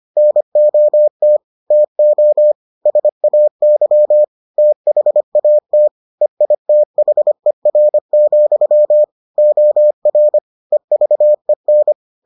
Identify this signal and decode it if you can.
Morse code, heard as a keyed tone, about 25 wpm, 600 Hz. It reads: NOT TO SAY THAT EITHER, OR EVEN